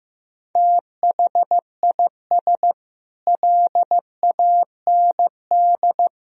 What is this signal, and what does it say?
Morse code, heard as a keyed tone, about 15 wpm, 700 Hz. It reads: THIS LAND
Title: Sentences from Top 1000 Words - Review Part 1 - 15wpm